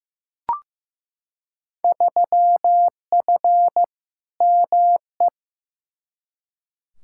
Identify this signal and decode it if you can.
Morse code, heard as a keyed tone, about 16 wpm, 700 Hz. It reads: 3F ME
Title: Field Day Exchanges - 15wpm